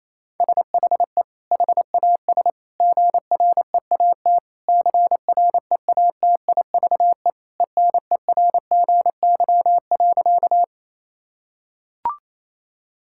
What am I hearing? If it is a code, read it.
Morse code, heard as a keyed tone, about 28 wpm, 700 Hz. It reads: SHE HAS GREAT CREATIVE ENERGY.
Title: Sentences from Top 1000 Words - Review Part 6 - 28wpm